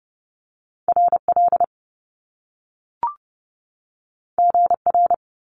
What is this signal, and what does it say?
Morse code, heard as a keyed tone, about 30 wpm, 700 Hz. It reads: RL GR
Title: Top Two Letter Combinations in Words 30wpm